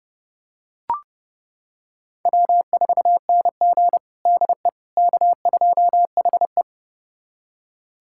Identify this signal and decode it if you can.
Morse code, heard as a keyed tone, about 30 wpm, 700 Hz. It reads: W4NG DE K2HE
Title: USA Callsign DE Callsign - 4 Characters - 30wpm